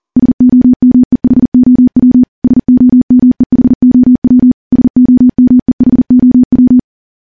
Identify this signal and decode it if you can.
Morse code, heard as a keyed tone, about 41 wpm, 250 Hz. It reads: SOMEHOW SOMEHOW SOMEHOW
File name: morse / morse-training-40WPM